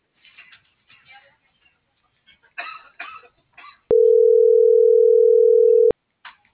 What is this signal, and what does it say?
Ringback tone (North America), 440 + 480 Hz.